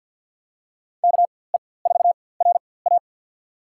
Morse code, heard as a keyed tone, about 53 wpm, spaced as for 24 wpm, 700 Hz. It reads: KE4RA
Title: USA Call Signs - 5 Characters - 4x Character Spacing - 50wpm